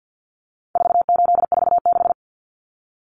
Morse code, heard as a keyed tone, about 50 wpm, 700 Hz. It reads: 4846
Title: Common Cut Numbers - 4-Digit 50wpm